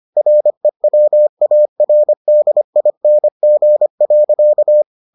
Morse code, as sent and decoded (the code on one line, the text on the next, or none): .-. .-- .- .-. -.. .. -. --. .-.-.-
REWARDING.